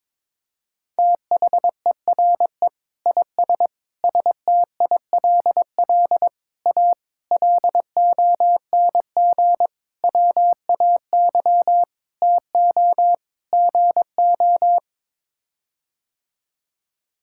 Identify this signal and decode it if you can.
Morse code, heard as a keyed tone, about 22 wpm, 700 Hz. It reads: THERE IS STILL A LONG WAY TO GO
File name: practice / 22wpm / Sets of 8 Words - 22wpm